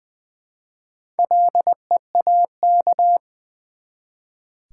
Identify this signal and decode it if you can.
Morse code, heard as a keyed tone, about 20 wpm, 700 Hz. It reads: LEAK
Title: Build A Word - 1 Letter At A Time - 4-Letter Words - 20wpm